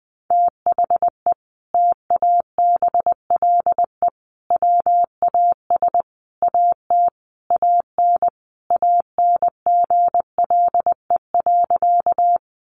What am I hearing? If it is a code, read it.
Morse code, heard as a keyed tone, about 20 wpm, 700 Hz. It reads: THE TABLE WAS AT AN ANGLE.
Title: Sentences from Top 1500 Words 20wpm